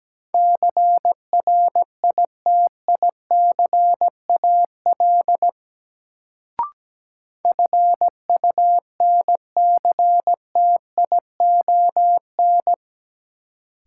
Morse code, heard as a keyed tone, about 17 wpm, 700 Hz. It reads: CRITICAL FUNCTION
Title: Sentences from Top 1100 Words - New Word Drill - 17wpm